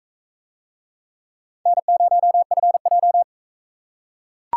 Morse code, heard as a keyed tone, about 40 wpm, 700 Hz. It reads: N0FJ